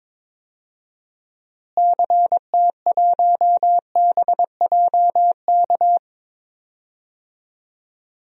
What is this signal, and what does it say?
Morse code, heard as a keyed tone, about 22 wpm, 700 Hz. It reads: CT1BJK
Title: DXCC-100 Call Signs - 6 Characters - Repeated 2x - 22wpm